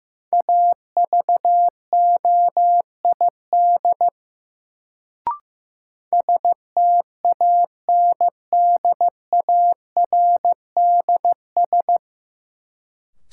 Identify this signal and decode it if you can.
Morse code, heard as a keyed tone, about 15 wpm, 700 Hz. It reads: AVOID STANDARDS
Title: Sentences from Top 1000 Words - Review Part 6 - New Word Drill - 15wpm